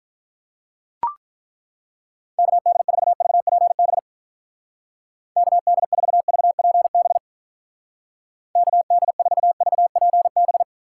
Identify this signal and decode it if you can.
Morse code, heard as a keyed tone, about 43 wpm, 700 Hz. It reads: KD4VPB KD4VPB KD4VPB
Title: Speed Racing - USA 2x3 Call Signs 35wpm